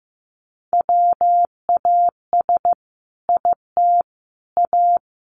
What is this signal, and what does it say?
Morse code, heard as a keyed tone, about 15 wpm, 700 Hz. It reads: WAS IT A
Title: Sentences from Top 1400 Words 15wpm